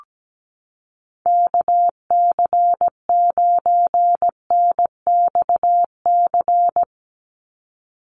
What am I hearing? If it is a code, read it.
Morse code, heard as a keyed tone, about 17 wpm, 700 Hz. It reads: KC9NXC
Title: USA Call Signs 17wpm